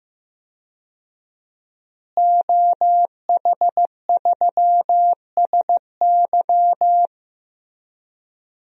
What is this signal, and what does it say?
Morse code, heard as a keyed tone, about 15 wpm, 700 Hz. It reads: OH3SY
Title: DXCC-100 Call Signs - 5 Characters - Repeated 2x - 15wpm